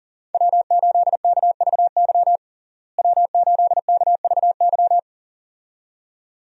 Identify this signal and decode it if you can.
Morse code, heard as a keyed tone, about 40 wpm, 700 Hz. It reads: W8KVY W8KVY
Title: USA Call Signs - 5 Characters - Repeated 2x - 40wpm